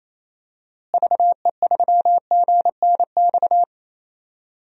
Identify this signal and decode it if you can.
Morse code, heard as a keyed tone, about 28 wpm, 700 Hz. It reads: VE3GNX
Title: DXCC-100 Call Signs - 6 Characters - Repeated 2x - 28wpm